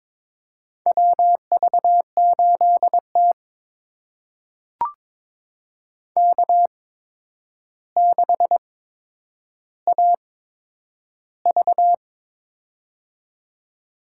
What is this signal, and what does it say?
Morse code, heard as a keyed tone, about 22 wpm, spaced as for 9 wpm, 700 Hz. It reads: WV8T K6AV
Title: USA Call Signs - 4 Characters - 8x Character Spacing - 22wpm